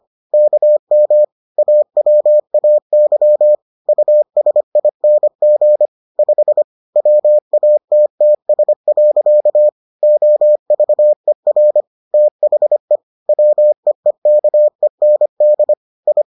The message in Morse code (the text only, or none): KM AWAY USING 5 WATTS. OVER THE WEEKEND I